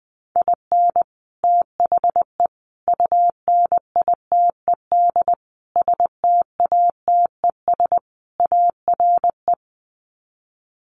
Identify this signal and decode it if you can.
Morse code, heard as a keyed tone, about 20 wpm, 700 Hz. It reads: IN THE UNITED STATES ARE
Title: Sets of 5 Words Encore 20wpm